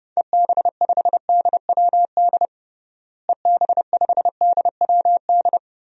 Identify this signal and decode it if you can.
Morse code, heard as a keyed tone, about 32 wpm, 700 Hz. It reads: EB5DWD EB5DWD
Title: DXCC-100 Call Signs - 6 Characters - Repeated 3x - 30wpm